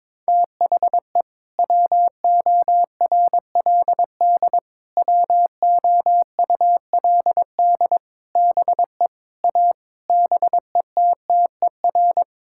Morse code, heard as a keyed tone, about 22 wpm, 700 Hz. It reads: THE WORLD WOULD BE A BETTER
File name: practice / 22wpm / Sets of 8 Words - 22wpm